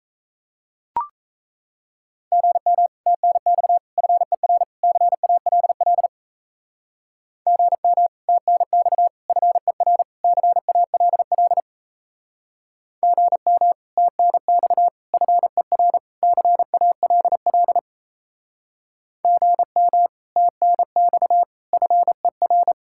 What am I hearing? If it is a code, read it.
Morse code, heard as a keyed tone, about 38 wpm, 700 Hz. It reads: GM TNX FER CALL GM TNX FER CALL GM TNX FER CALL GM TNX FER